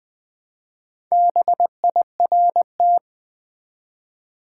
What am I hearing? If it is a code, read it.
Morse code, heard as a keyed tone, about 20 wpm, 700 Hz. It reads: BIRT
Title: Build A Word - 1 Letter At A Time - 3 to 5-Letter Words - 20wpm